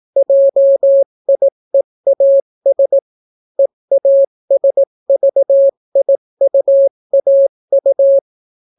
Morse code, heard as a keyed tone, about 18 wpm, 550 Hz. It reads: JIEAS EASVIUAU